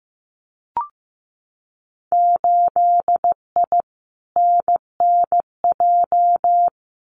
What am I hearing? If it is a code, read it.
Morse code, heard as a keyed tone, about 15 wpm, 700 Hz. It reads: 8I NNJ
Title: Winter Field Day Exchanges Encore 15wpm